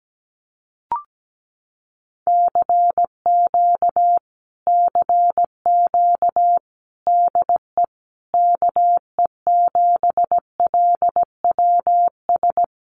Morse code, heard as a keyed tone, about 17 wpm, 700 Hz. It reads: CQ CQ DE KE7LWS
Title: Common QSO Phrases Encore 17wpm